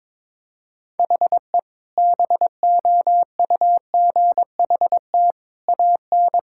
Morse code, heard as a keyed tone, about 22 wpm, 700 Hz. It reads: HE BOUGHT AN
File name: practice / 22wpm / Sentences with 2-Word Phrases - 22wpm